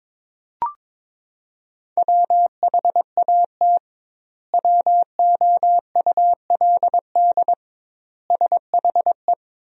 Morse code, heard as a keyed tone, about 22 wpm, 700 Hz. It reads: WHAT WOULD SHE
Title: 2x Word-Spacing - Sentences from Top 100 Words - 22wpm